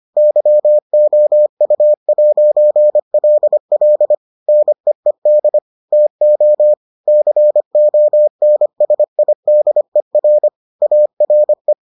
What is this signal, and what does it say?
Morse code, heard as a keyed tone, about 25 wpm, 600 Hz. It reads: YOU'LL NEED TO CONSIDER ARE